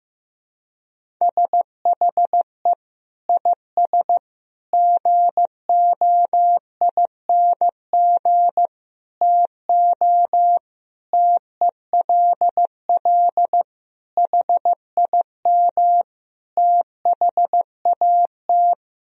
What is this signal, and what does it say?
Morse code, heard as a keyed tone, about 15 wpm, 700 Hz. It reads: SHE IS GOING TO TELL HIM THAT
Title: Sentences from Top 1000 Words - Review Part 1 - 15wpm